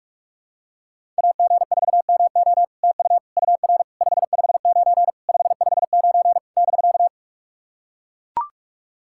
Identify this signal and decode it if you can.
Morse code, heard as a keyed tone, about 44 wpm, 700 Hz. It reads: AG4MO TU UR 559 559 <BK>
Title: SOTA QSO Practice 45wpm